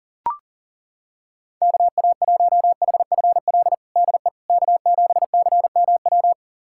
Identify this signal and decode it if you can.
Morse code, heard as a keyed tone, about 37 wpm, 700 Hz. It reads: KA1HFL DE K7CMW
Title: USA Callsign DE Callsign 40wpm